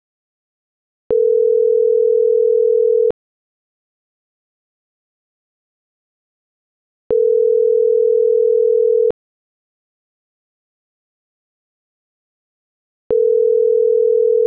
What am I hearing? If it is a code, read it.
Ringback tone (North America), 440 + 480 Hz.